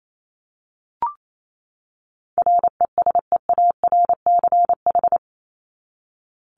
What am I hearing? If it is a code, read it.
Morse code, heard as a keyed tone, about 28 wpm, 700 Hz. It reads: RESEARCH
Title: Top 501-750 Words 28wpm